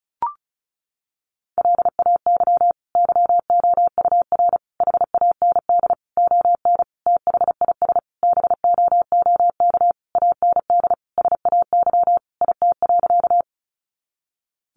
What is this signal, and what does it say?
Morse code, heard as a keyed tone, about 34 wpm, 700 Hz. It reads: LAY YOUR HAND ON THIS BOOK AND SAY IT.
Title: Sentences from Top 400 Words 35wpm